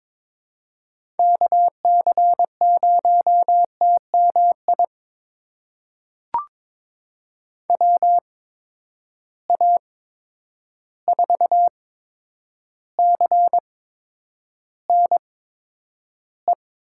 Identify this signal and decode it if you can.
Morse code, heard as a keyed tone, about 22 wpm, spaced as for 9 wpm, 700 Hz. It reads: KC0TMI WA4CNE
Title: USA Call Signs - 6 Characters - 8x Character Spacing - 22wpm